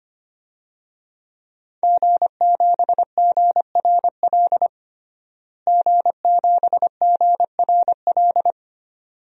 Morse code, heard as a keyed tone, about 25 wpm, 700 Hz. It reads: G7GRL G7GRL
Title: DXCC-100 Call Signs - 5 Characters - Repeated 2x - 25wpm